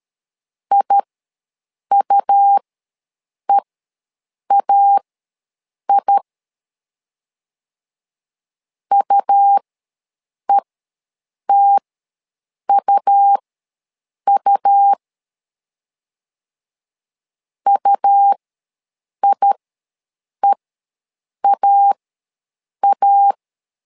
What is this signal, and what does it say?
Morse code, heard as a keyed tone, about 13 wpm, spaced as for 7 wpm, 800 Hz. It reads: IUEAI UETUU UIEAA
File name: LEZIONI / 2 UHDN / 2 U ws 12wpm 1655